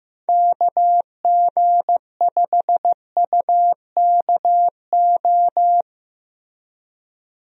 Morse code, heard as a keyed tone, about 15 wpm, 700 Hz. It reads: KG5UKO
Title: USA Call Signs - 6 Characters - Repeated 3x - 15wpm